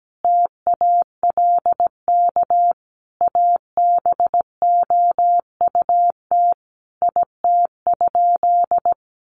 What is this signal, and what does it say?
Morse code, heard as a keyed tone, about 17 wpm, 700 Hz. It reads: TALK ABOUT IT?
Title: Sentences from Top 1400 Words 17wpm